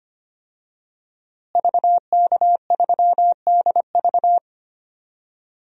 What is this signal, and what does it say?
Morse code, heard as a keyed tone, about 25 wpm, 700 Hz. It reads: VK3DV
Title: DXCC-100 Call Signs - 5 Characters - Repeated 2x - 25wpm